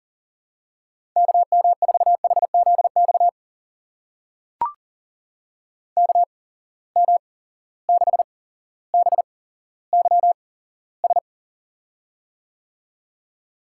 Morse code, heard as a keyed tone, about 40 wpm, spaced as for 17 wpm, 700 Hz. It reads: KM4HZX KM6BYS
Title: USA Call Signs - 6 Characters - 8x Character Spacing - 40wpm